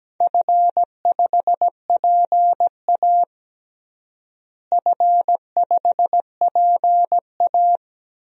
Morse code, heard as a keyed tone, about 17 wpm, 700 Hz. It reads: F5PA F5PA
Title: DXCC-100 Call Signs - 4 Characters - Repeated 2x - 17wpm